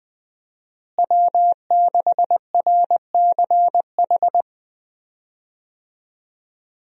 Morse code, heard as a keyed tone, about 20 wpm, 700 Hz. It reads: W6RCH
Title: USA Call Signs - 5 Characters - Repeated 2x - 20wpm